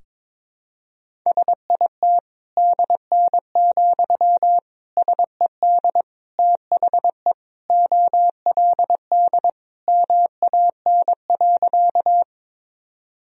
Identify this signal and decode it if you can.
Morse code, heard as a keyed tone, about 22 wpm, 700 Hz. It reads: SIT DN, SED THE OLD MAN.
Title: Sentences from Top 500 Words and Abbreviations 22wpm